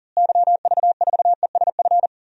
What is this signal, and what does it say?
Morse code, heard as a keyed tone, about 40 wpm, 700 Hz. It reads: YV4ESF
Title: DXCC-100 Call Signs - 6 Characters - Repeated 3x - 40wpm